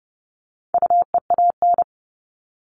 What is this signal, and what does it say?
Morse code, heard as a keyed tone, about 30 wpm, 700 Hz. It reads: UEAN
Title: All Cut Numbers - 4-Digit 30wpm